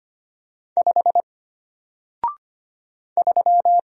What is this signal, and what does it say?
Morse code, heard as a keyed tone, about 25 wpm, 700 Hz. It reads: 5 3
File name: Single-Letter-Number-25wpm